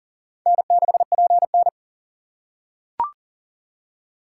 Morse code, heard as a keyed tone, about 38 wpm, 700 Hz. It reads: N6PN
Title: USA Call Signs - 4 Characters - 2x Character Spacing - 40wpm